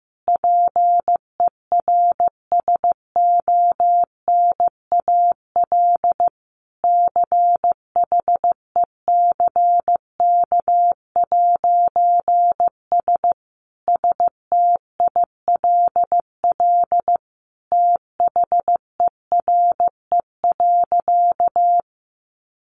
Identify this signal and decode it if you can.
Morse code, heard as a keyed tone, about 15 wpm, 700 Hz. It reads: PERSONAL CHECK'S STILL THERE.
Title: A Sound Of Thunder 15wpm